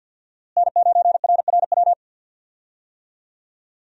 Morse code, heard as a keyed tone, about 52 wpm, 700 Hz. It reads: N9RRW
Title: USA Call Signs - 5 Characters - Repeated 2x - 50wpm